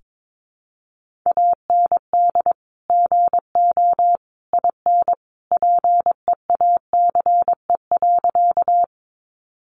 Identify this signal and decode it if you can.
Morse code, heard as a keyed tone, about 22 wpm, 700 Hz. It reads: AND GO IN PEACE.
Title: Sentences from Top 1500 Words 22wpm